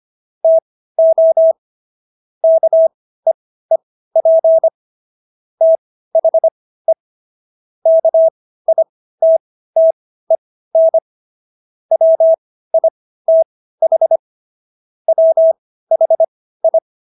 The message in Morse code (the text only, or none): TO KEEP THE KITTEN WITH WHI